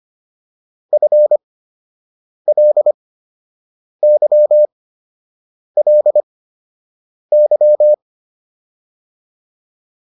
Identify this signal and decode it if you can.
Morse code, heard as a keyed tone, about 25 wpm, spaced as for 7 wpm, 600 Hz. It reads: FLYLY